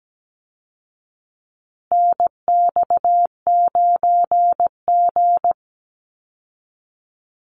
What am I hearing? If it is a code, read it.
Morse code, heard as a keyed tone, about 17 wpm, 700 Hz. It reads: NX9G
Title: Call Signs Repeated 3x - 17wpm